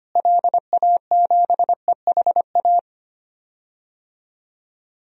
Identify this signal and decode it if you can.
Morse code, heard as a keyed tone, about 25 wpm, 700 Hz. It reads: LA7EHA
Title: DXCC-100 Call Signs - 6 Characters - Repeated 3x - 25wpm